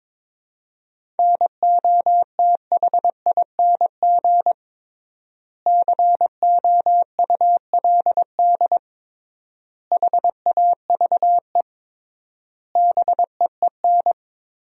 Morse code, heard as a keyed tone, about 21 wpm, 700 Hz. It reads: NOTHING COULD HAVE BEEN
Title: 3x Word-Spacing - Sentences from Top 500 Words - 22wpm